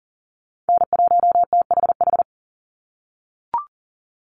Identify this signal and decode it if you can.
Morse code, heard as a keyed tone, about 40 wpm, 700 Hz. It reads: N1THH
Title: USA Call Signs 40wpm